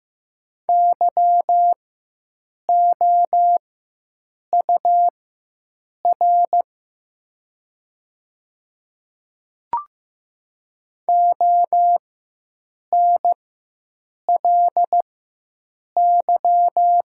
Morse code, heard as a keyed tone, about 15 wpm, spaced as for 7 wpm, 700 Hz. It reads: YOUR ONLY
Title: 4x Character Spacing - Top 200 Words - 15wpm